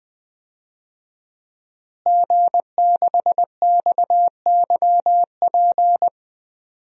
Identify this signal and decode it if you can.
Morse code, heard as a keyed tone, about 20 wpm, 700 Hz. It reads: G6XYP
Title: DXCC-100 Call Signs - 5 Characters - Repeated 3x - 20wpm